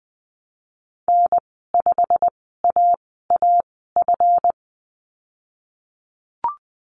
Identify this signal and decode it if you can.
Morse code, heard as a keyed tone, about 20 wpm, spaced as for 15 wpm, 700 Hz. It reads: N5AAF
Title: 154 Lesson 25g - Letter p - Call Signs 20wpm